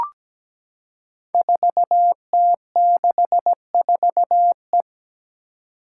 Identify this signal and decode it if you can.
Morse code, heard as a keyed tone, about 17 wpm, 700 Hz. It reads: 4T64E
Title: Five Challenging Letter-Number Combinations 17wpm